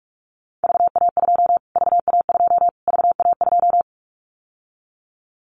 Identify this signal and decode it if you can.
Morse code, heard as a keyed tone, about 44 wpm, 700 Hz. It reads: VA2 VA2 VA2
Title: Callsign Prefixes for Top 25 DXCC - Repeated 3x - 45wpm